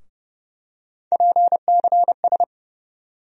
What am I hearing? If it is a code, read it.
Morse code, heard as a keyed tone, about 30 wpm, 700 Hz. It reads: PCS
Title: USA Call Sign Postfixes - 3 Letters 30wpm